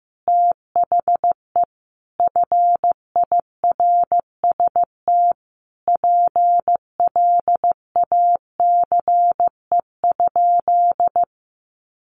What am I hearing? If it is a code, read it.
Morse code, heard as a keyed tone, about 15 wpm, 700 Hz. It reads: THE FIRST PLACE?
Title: Sentences from Top 200 Words 15wpm